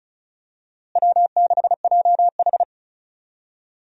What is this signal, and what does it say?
Morse code, heard as a keyed tone, about 36 wpm, 700 Hz. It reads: W6JH
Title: USA Call Signs - 4 Characters - Repeated 3x - 35wpm